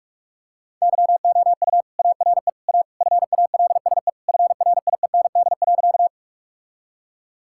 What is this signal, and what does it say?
Morse code, heard as a keyed tone, about 47 wpm, 700 Hz. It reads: YOU ARE A FALSE FRIEND.
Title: Sentences from Top 1600 Words 45wpm